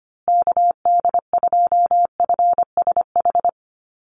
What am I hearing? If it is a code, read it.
Morse code, heard as a keyed tone, about 25 wpm, 700 Hz. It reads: KD2FSH
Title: Call Signs Repeated 2x - 25wpm